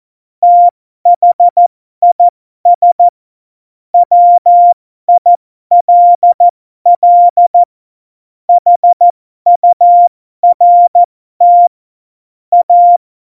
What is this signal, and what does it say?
Morse code, heard as a keyed tone, about 14 wpm, spaced as for 12 wpm, 700 Hz. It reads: THIS WILL HURT A
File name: Morsekurs / Witze